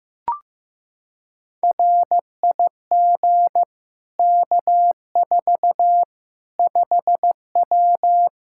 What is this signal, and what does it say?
Morse code, heard as a keyed tone, about 15 wpm, 700 Hz. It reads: RIG K4 5W
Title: Practice QSO - Exchange 2 - Info - 15wpm